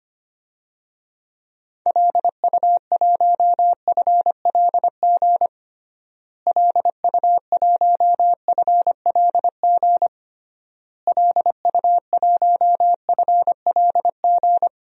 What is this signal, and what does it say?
Morse code, heard as a keyed tone, about 25 wpm, 700 Hz. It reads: LU1FLG LU1FLG LU1FLG
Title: DXCC-100 Call Signs - 6 Characters - Repeated 3x - 25wpm